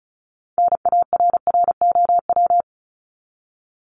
Morse code, heard as a keyed tone, about 34 wpm, 700 Hz. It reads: NARROW
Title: Top 1501-1750 Words 35wpm